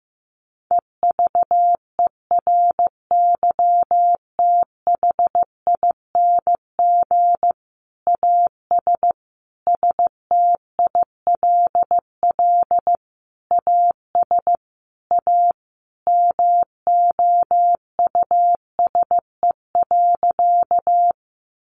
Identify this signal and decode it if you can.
Morse code, heard as a keyed tone, about 14 wpm, 700 Hz. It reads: EVERYTHING AS STILL AS A MOUSE.
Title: Sentences from Top 700 Words 15wpm